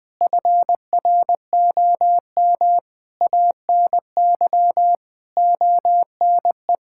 Morse code, heard as a keyed tone, about 20 wpm, 700 Hz. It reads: FROM ANY ONE
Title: Sentences from Top 1000 Words - Review Part 5 - 20wpm